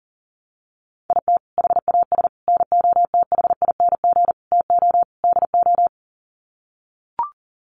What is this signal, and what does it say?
Morse code, heard as a keyed tone, about 40 wpm, 700 Hz. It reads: IT HAS NOTHING TO DO
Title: Sets of 5 Words Encore 40wpm